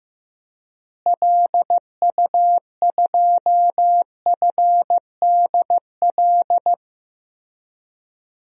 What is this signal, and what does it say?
Morse code, heard as a keyed tone, about 15 wpm, 700 Hz. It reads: LU2FDL
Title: DXCC-100 Call Signs - 6 Characters - Repeated 3x - 15wpm